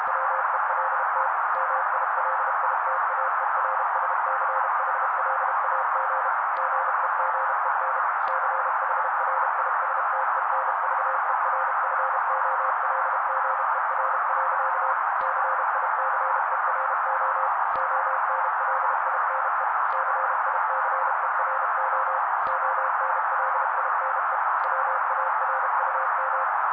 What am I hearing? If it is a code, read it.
Morse code, heard as a keyed tone, about 32 wpm, 530 Hz. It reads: WELT MILITAERSCHLAG GEGEN CHRISTENVERFOLGER. CICERO ONLINE CICERO ONLINE WARUM
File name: daily / 31wpm noise